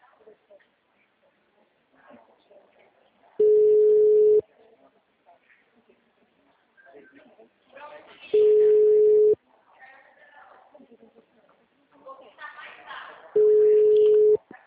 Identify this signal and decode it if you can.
Ringback tone (Europe, 425 Hz), with speech.
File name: out- -2003- - - .19157